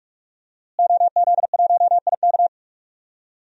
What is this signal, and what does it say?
Morse code, heard as a keyed tone, about 41 wpm, 700 Hz. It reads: OZ1IK
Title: DXCC-100 Call Signs - 5 Characters - Repeated 2x - 45wpm